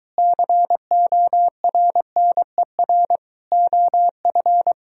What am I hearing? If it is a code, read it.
Morse code, heard as a keyed tone, about 23 wpm, 700 Hz. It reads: CORNER OF